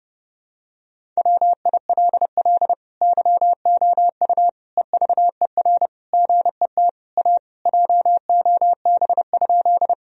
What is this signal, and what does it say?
Morse code, heard as a keyed tone, about 30 wpm, 700 Hz. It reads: WILL YOU EVER GET A JOB?
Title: Sentences from Top 1000 Words - Review Part 3 - 30wpm